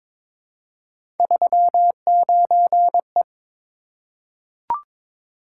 Morse code, heard as a keyed tone, about 22 wpm, 700 Hz. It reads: 39E
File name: practice / 22wpm / Three Letter-Number Combinations - 22wpm